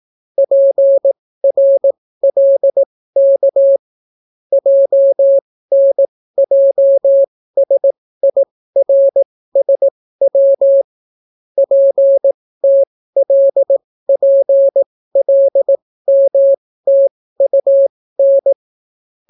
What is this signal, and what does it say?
Morse code, heard as a keyed tone, about 18 wpm, spaced as for 14 wpm, 550 Hz. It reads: PRLK JNJSIRSW PTLPLMTUN